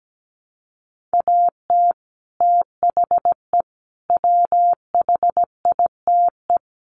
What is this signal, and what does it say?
Morse code, heard as a keyed tone, about 17 wpm, 700 Hz. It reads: AT THE WHITE
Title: Sets of 4 Words 17wpm